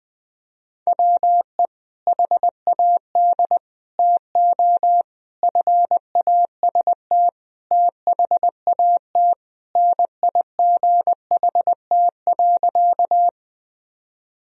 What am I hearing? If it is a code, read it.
Morse code, heard as a keyed tone, about 21 wpm, 700 Hz. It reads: WE HAD TO FAST THAT NIGHT.
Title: Sentences from Top 1000 Words - Review Part 2 - 20wpm